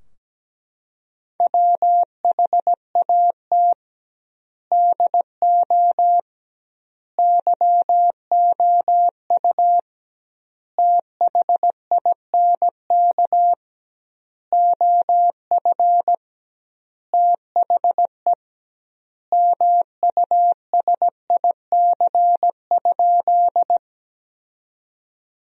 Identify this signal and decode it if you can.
Morse code, heard as a keyed tone, about 17 wpm, 700 Hz. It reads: WHAT DO YOU THINK OF THE MUSIC?
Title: 2x Word-Spacing - Sentences from Top 300 Words - 17wpm